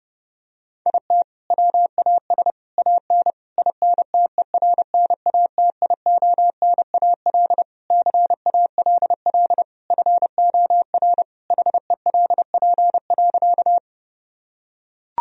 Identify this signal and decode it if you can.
Morse code, heard as a keyed tone, about 30 wpm, 700 Hz. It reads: IT WAS AN INTERNATIONAL CALL FOR HELP.
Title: Sentences from Top 1000 Words - Review Part 4 - 30wpm